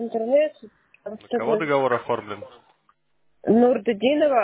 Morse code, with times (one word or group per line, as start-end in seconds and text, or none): none